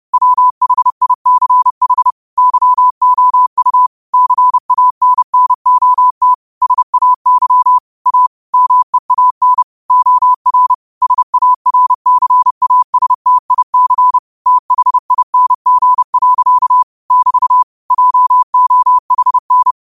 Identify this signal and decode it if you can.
Morse code, heard as a keyed tone, about 30 wpm, 1000 Hz. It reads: WHICH YOU CANNOT SAY A MEAN OR SARCASTIC THING. <BT> JOHN
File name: qotdfiles / 30wpm